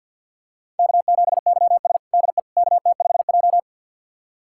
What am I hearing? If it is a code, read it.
Morse code, heard as a keyed tone, about 49 wpm, 700 Hz. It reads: K7YS DE KT5J
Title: USA Callsign DE Callsign - 4 Characters - 50wpm